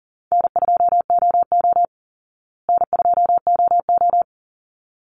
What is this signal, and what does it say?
Morse code, heard as a keyed tone, about 37 wpm, 700 Hz. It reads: N2OO N2OO
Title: Easy Call Signs Repeated 2x - 40wpm